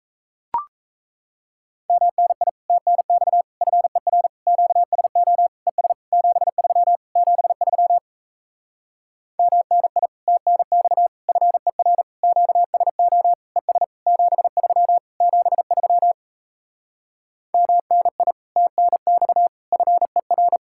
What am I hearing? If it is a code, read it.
Morse code, heard as a keyed tone, about 38 wpm, 700 Hz. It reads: MNI TNX FER QSO ES 73 73 MNI TNX FER QSO ES 73 73 MNI TNX FER